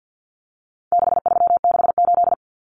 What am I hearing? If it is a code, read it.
Morse code, heard as a keyed tone, about 48 wpm, 700 Hz. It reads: 6368
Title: Common Cut Numbers - 4-Digit 50wpm